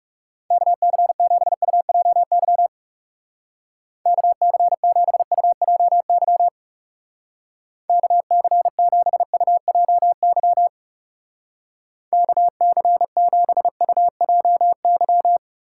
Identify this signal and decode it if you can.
Morse code, heard as a keyed tone, about 36 wpm, 700 Hz. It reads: KC7UJY KC7UJY KC7UJY KC7UJY
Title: Speed Racing - USA 2x3 Call Signs 30wpm